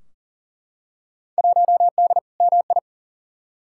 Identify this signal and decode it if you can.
Morse code, heard as a keyed tone, about 40 wpm, 700 Hz. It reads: 1D MI
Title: Field Day Exchanges - 40wpm